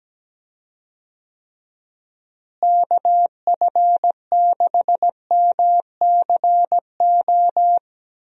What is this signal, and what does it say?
Morse code, heard as a keyed tone, about 17 wpm, 700 Hz. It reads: KF6MCO
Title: USA Call Signs - 6 Characters - Repeated 3x - 17wpm